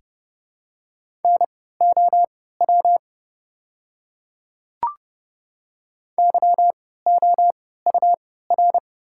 Morse code, heard as a keyed tone, about 32 wpm, spaced as for 17 wpm, 700 Hz. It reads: NOW YOUR
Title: 3x Character Spacing - Top 100 Words - 30wpm